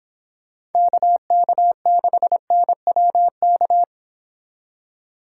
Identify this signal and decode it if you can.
Morse code, heard as a keyed tone, about 29 wpm, 700 Hz. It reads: KK6NWK